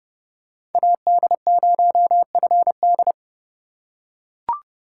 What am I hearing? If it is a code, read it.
Morse code, heard as a keyed tone, about 30 wpm, 700 Hz. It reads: AD0FD
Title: USA Call Signs - 4x Character Spacing - 30wpm